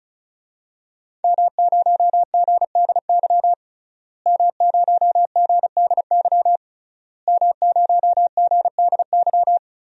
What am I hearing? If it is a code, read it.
Morse code, heard as a keyed tone, about 34 wpm, 690 Hz. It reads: M0GDY M0GDY M0GDY
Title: DXCC-100 Call Signs - 5 Characters - Repeated 3x - 35wpm